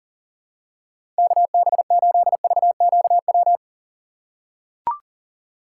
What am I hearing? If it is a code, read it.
Morse code, heard as a keyed tone, about 40 wpm, 700 Hz. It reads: KB8VQW
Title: USA Call Signs - 6 Characters - 4x Character Spacing - 40wpm